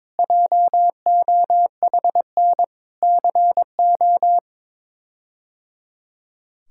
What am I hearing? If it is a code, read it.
Morse code, heard as a keyed tone, about 22 wpm, 700 Hz. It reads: JOHN CO
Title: K1USN SST QSO Practice - 22wpm